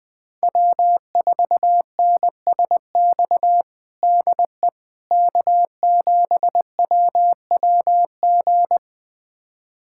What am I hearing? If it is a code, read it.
Morse code, heard as a keyed tone, about 21 wpm, 700 Hz. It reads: W4NSX DE K7WWG
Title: USA Callsign DE Callsign 20wpm